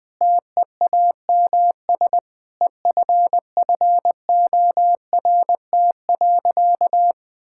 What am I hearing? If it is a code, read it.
Morse code, heard as a keyed tone, about 20 wpm, 700 Hz. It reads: TEAMS EFFORT.
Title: Sentences with 7-Word Phrases - 20wpm